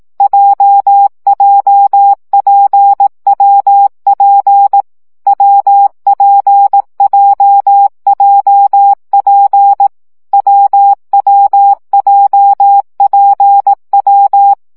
Morse code, heard as a keyed tone, about 18 wpm, 800 Hz. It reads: JJPWP WPJJP WWJPW
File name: CW gyakorlas w p j 15 baud